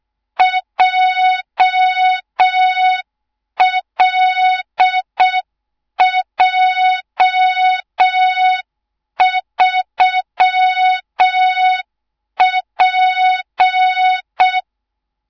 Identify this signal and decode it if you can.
Morse code, heard as a keyed tone, about 6 wpm, 3700 Hz, with noise. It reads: JLJ3P
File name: Morze-HA2MN / hang / 24 GYAKORLAT